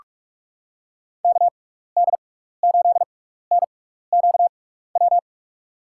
Morse code, heard as a keyed tone, about 43 wpm, spaced as for 15 wpm, 700 Hz. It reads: KD8NQW
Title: USA Call Signs - 6 Characters - 6x Character Spacing - 45wpm